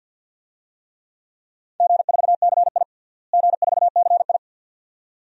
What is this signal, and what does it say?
Morse code, heard as a keyed tone, about 45 wpm, 700 Hz. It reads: G4CI G4CI
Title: DXCC-100 Call Signs - 4 Characters - Repeated 2x - 50wpm